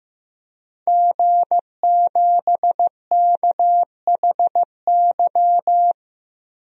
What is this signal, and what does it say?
Morse code, heard as a keyed tone, about 15 wpm, 700 Hz. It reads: G7KHY
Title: DXCC-100 Call Signs - 5 Characters - 15wpm